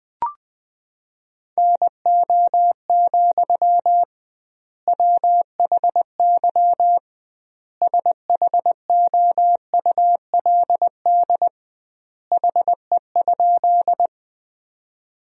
Morse code, heard as a keyed tone, about 20 wpm, 700 Hz. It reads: NO, WHY SHOULD HE?
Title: 2x Word-Spacing - Sentences from Top 300 Words - 20wpm